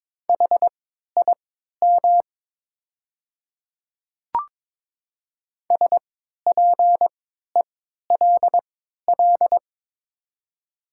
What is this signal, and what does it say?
Morse code, heard as a keyed tone, about 22 wpm, spaced as for 12 wpm, 700 Hz. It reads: HIM SPELL
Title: 3x Character Spacing - Top 300 Words - 22wpm